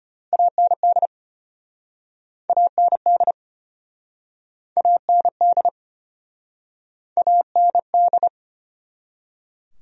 Morse code, heard as a keyed tone, about 35 wpm, 700 Hz. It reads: AND AND AND AND